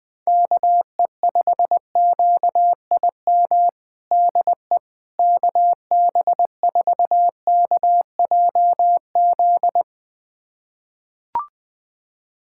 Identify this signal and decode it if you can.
Morse code, heard as a keyed tone, about 21 wpm, 700 Hz. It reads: KE5QIM DE KB4KJZ
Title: USA Callsign DE Callsign - 6 Characters - 20wpm